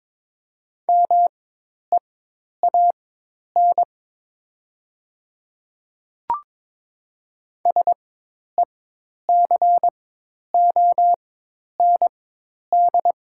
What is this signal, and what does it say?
Morse code, heard as a keyed tone, about 22 wpm, spaced as for 10 wpm, 700 Hz. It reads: MEAN SECOND
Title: 4x Character Spacing - Top 500 Words - 22wpm